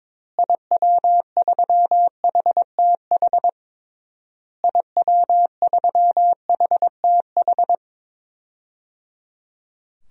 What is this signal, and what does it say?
Morse code, heard as a keyed tone, about 22 wpm, 700 Hz. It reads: IW3HTH IW3HTH
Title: DXCC-100 Call Signs - 6 Characters - Repeated 3x - 22wpm